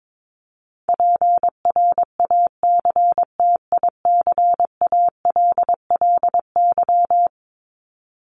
Morse code, heard as a keyed tone, about 22 wpm, 700 Hz. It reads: PRACTICALLY
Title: LY Suffix - Top 100 Words with Base Words 22wpm